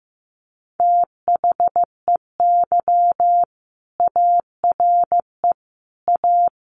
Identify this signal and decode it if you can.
Morse code, heard as a keyed tone, about 15 wpm, 700 Hz. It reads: THEY ARE A
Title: Sentences from Top 1200 Words 15wpm